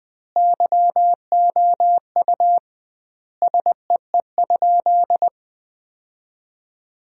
Morse code, heard as a keyed tone, about 20 wpm, 700 Hz. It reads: YOU SEE?
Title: 2x Word-Spacing - Sentences from Top 100 Words - 20wpm